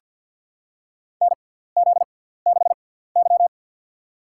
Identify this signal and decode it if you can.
Morse code, heard as a keyed tone, about 46 wpm, spaced as for 17 wpm, 700 Hz. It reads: NZ6Y
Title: USA Call Signs - 4 Characters - 6x Character Spacing - 50wpm